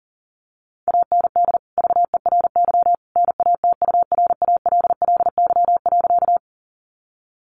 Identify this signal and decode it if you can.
Morse code, heard as a keyed tone, about 40 wpm, 700 Hz. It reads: AND VERY NATURALLY.
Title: Sentences from Top 1300 Words 40wpm